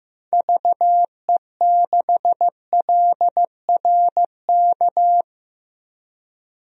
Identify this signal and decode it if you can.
Morse code, heard as a keyed tone, about 15 wpm, 700 Hz. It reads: VE6LRK
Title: DXCC-100 Call Signs - 6 Characters - 15wpm